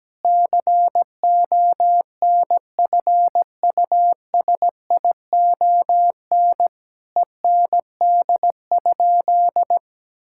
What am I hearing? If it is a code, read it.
Morse code, heard as a keyed tone, about 17 wpm, 700 Hz. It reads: CONFUSION END?
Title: Sentences from Top 1600 Words 17wpm